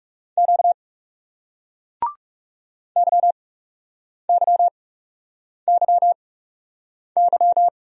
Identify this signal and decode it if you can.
Morse code, heard as a keyed tone, about 39 wpm, spaced as for 8 wpm, 700 Hz. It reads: Q YYYY